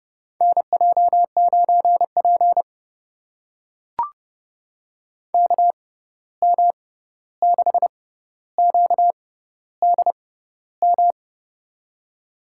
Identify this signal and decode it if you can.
Morse code, heard as a keyed tone, about 30 wpm, spaced as for 14 wpm, 700 Hz. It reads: NJ9P KM6QDM
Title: USA Call Signs - 6x Character Spacing - 30wpm